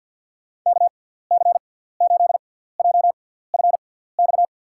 Morse code, heard as a keyed tone, about 49 wpm, spaced as for 17 wpm, 700 Hz. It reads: KC8JFX
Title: USA Call Signs - 6 Characters - 6x Character Spacing - 50wpm